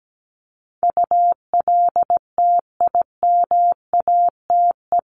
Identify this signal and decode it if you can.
Morse code, heard as a keyed tone, about 17 wpm, 700 Hz. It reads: ULTIMATE
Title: LY Suffix - Top 100 Words with Base Words 17wpm